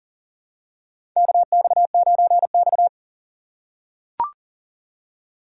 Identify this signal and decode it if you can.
Morse code, heard as a keyed tone, about 39 wpm, 700 Hz. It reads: KX9X
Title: USA Call Signs - 4x Character Spacing - 40wpm